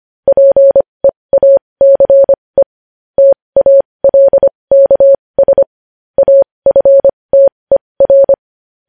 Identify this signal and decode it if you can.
Morse code, heard as a keyed tone, about 25 wpm, spaced as for 20 wpm, 550 Hz. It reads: PEACE TALKS AFTER
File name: podcast / News headlines-1x-25x20